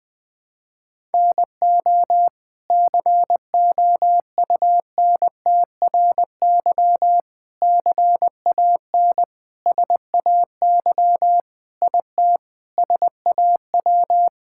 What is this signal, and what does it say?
Morse code, heard as a keyed tone, about 20 wpm, 700 Hz. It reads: NO COUNTRY CAN SAY IT SAW
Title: Sentences from Top 1000 Words - Review Part 5 - 20wpm